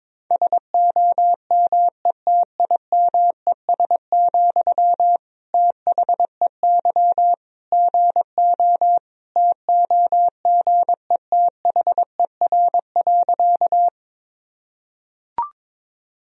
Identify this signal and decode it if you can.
Morse code, heard as a keyed tone, about 22 wpm, 700 Hz. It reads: SOMETIMES, THEY GO TOGETHER.
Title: Sentences from Top 1000 Words - Review Part 3 - 22wpm